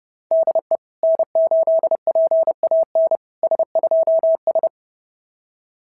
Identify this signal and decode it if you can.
Morse code, heard as a keyed tone, about 30 wpm, 650 Hz. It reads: DE N8PAN S2S
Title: SOTA QSO Practice 30wpm